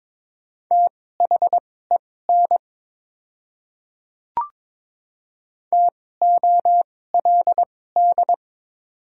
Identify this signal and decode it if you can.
Morse code, heard as a keyed tone, about 22 wpm, spaced as for 16 wpm, 700 Hz. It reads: THEN TOLD
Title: 2x Character Spacing - Top 500 Words - 22wpm